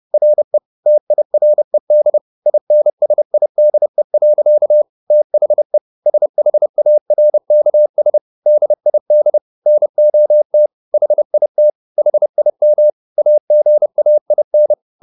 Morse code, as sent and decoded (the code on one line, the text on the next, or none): .-. / - .. .-. -.. / .. -. ... .. -.. .-.-.- / - .... / ... .... .- .-. -.- ... / -.. .. -.. / -. --- - / .... .. - / .... .. -- / .- --. .- .. -.
RE TIRED INSIDE. THE SHARKS DID NOT HIT HIM AGAIN